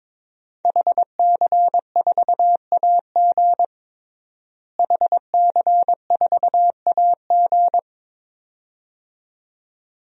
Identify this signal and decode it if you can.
Morse code, heard as a keyed tone, about 22 wpm, 700 Hz. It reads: HC4AG HC4AG
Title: DXCC-100 Call Signs - 5 Characters - Repeated 2x - 22wpm